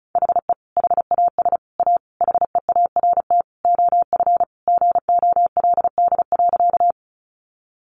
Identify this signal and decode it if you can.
Morse code, heard as a keyed tone, about 34 wpm, 700 Hz. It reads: HE HAS A HEART OF GOLD.
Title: Sentences from Top 1200 Words 35wpm